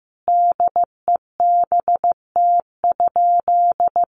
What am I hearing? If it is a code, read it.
Morse code, heard as a keyed tone, about 15 wpm, 700 Hz. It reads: DEBT?
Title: Sentences from Top 1300 Words 15wpm